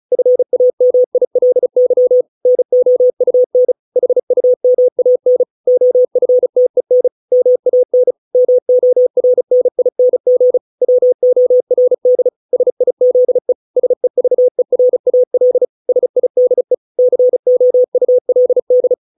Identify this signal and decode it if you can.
Morse code, heard as a keyed tone, about 34 wpm, 500 Hz. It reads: FAMILY NOUN HUMAN OFTEN MAN MORNING WORD SIZE SEVERAL SIDE COULD